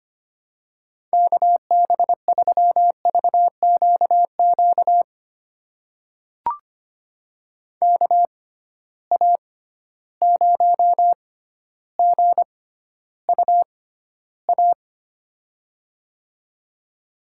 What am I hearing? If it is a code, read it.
Morse code, heard as a keyed tone, about 25 wpm, spaced as for 13 wpm, 700 Hz. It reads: KB3VQQ KA0GUA
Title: USA Call Signs - 6 Characters - 6x Character Spacing - 25wpm